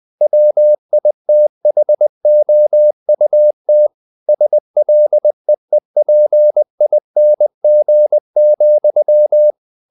Morse code, heard as a keyed tone, about 20 wpm, 600 Hz. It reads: WITHOUT SLEEPING,
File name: the old man and the sea part 2 20wpm